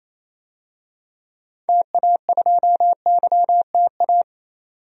Morse code, heard as a keyed tone, about 28 wpm, 700 Hz. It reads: TA2YTA
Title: DXCC-100 Call Signs - 6 Characters - 28wpm